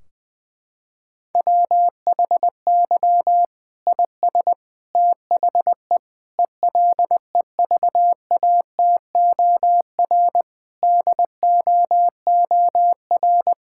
Morse code, heard as a keyed tone, about 20 wpm, 700 Hz. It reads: WHY IS THE ELEVATOR DOOR
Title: Sentences from Top 1600 Words 20wpm